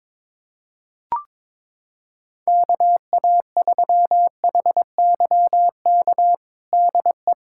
Morse code, heard as a keyed tone, about 22 wpm, 700 Hz. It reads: KA3HYK DE
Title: USA Callsign DE Callsign - 6 Characters - 22wpm